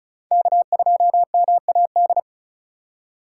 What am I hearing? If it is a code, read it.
Morse code, heard as a keyed tone, about 34 wpm, 700 Hz. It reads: K2MAD
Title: USA Call Signs - 5 Characters - 4x Character Spacing - 35wpm